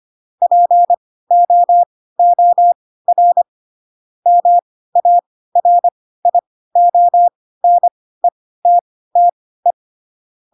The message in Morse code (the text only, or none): POOR MARIONETTE